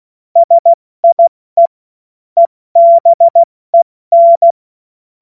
Morse code, heard as a keyed tone, about 16 wpm, 680 Hz. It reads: SIE EBEN
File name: Morsekurs / Witze / Witz 04. Juli 2023 70 F80